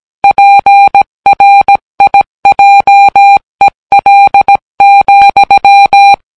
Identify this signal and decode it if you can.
Morse code, heard as a keyed tone, about 17 wpm, 800 Hz. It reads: PRIJEL,